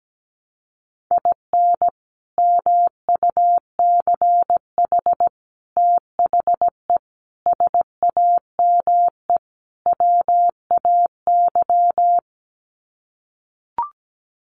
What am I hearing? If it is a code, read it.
Morse code, heard as a keyed tone, about 17 wpm, 700 Hz. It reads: IN MUCH THE SAME WAY
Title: Sets of 5 Words Encore 17wpm